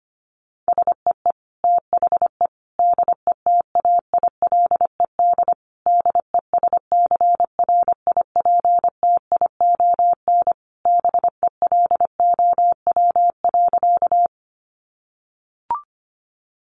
Morse code, heard as a keyed tone, about 25 wpm, 700 Hz. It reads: SEE THE DETAILED DESCRIPTION BELOW.
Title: Sentences from Top 1500 Words 25wpm